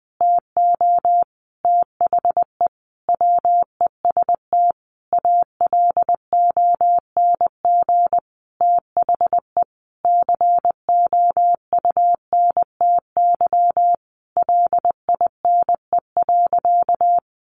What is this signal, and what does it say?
Morse code, heard as a keyed tone, about 20 wpm, 700 Hz. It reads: TO THE WEST ALONG THE COUNTY LINE.
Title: Sentences from Top 1200 Words 20wpm